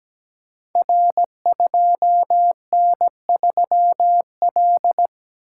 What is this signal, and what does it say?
Morse code, heard as a keyed tone, about 17 wpm, 700 Hz. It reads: R2N3L
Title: Code Groups - 5 Letter:Numbers - 17wpm